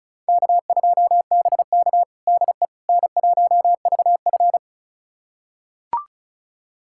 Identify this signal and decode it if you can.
Morse code, heard as a keyed tone, about 34 wpm, 700 Hz. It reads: K2BK DE N1VF
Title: USA Callsign DE Callsign - 4 Characters - 35wpm